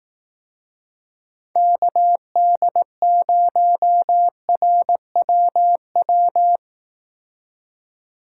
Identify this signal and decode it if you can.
Morse code, heard as a keyed tone, about 18 wpm, 700 Hz. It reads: KD0RWW
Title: Speed Racing - USA 2x3 Call Signs 15wpm